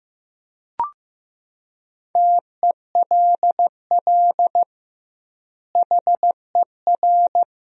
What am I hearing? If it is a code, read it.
Morse code, heard as a keyed tone, about 15 wpm, 700 Hz. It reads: TELL HER
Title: 2x Word-Spacing - Sentences from Top 300 Words - 15wpm